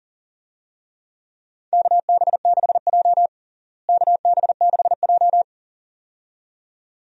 Morse code, heard as a keyed tone, about 40 wpm, 700 Hz. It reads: KB6J KB6J
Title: USA Call Signs - 4 Characters - Repeated 2x - 40wpm